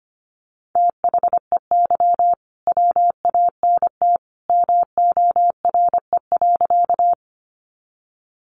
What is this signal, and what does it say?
Morse code, heard as a keyed tone, about 25 wpm, 700 Hz. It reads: THEY WANT MORE.
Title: Sentences from Top 200 Words 25wpm